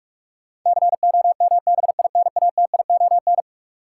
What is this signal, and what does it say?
Morse code, heard as a keyed tone, about 46 wpm, 700 Hz. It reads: COMBINATION
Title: Sentences from Top 1000 Words - Review Part 6 - New Word Drill - 45wpm